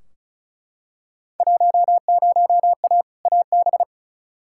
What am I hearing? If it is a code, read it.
Morse code, heard as a keyed tone, about 36 wpm, 690 Hz. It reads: 10A AB
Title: Field Day Exchanges Encore - 35wpm